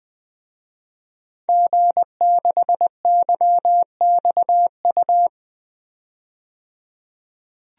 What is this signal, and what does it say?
Morse code, heard as a keyed tone, about 20 wpm, 700 Hz. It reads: G6YXU
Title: DXCC-100 Call Signs - 5 Characters - 20wpm